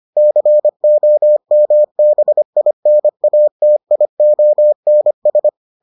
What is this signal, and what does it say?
Morse code, heard as a keyed tone, about 25 wpm, 600 Hz. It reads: COMBINATIONS